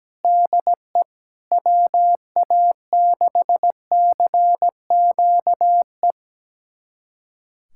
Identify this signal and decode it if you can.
Morse code, heard as a keyed tone, about 17 wpm, 700 Hz. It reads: DE WA6CQE
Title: USA Callsign DE Callsign 17wpm